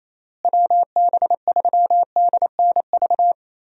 Morse code, heard as a keyed tone, about 28 wpm, 700 Hz. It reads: WB3DNV
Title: USA Call Signs - 6 Characters - Repeated 2x - 28wpm